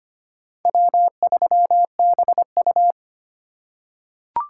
Morse code, heard as a keyed tone, about 25 wpm, 700 Hz. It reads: W3BU
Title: USA Call Signs - 4 Characters - 2x Character Spacing - 25wpm